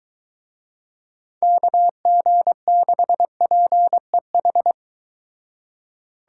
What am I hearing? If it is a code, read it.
Morse code, heard as a keyed tone, about 23 wpm, 700 Hz. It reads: KG6PEH